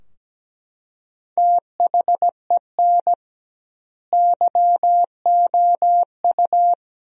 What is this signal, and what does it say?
Morse code, heard as a keyed tone, about 17 wpm, 700 Hz. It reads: THEN YOU
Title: 2x Word-Spacing - Sentences from Top 400 Words - 17wpm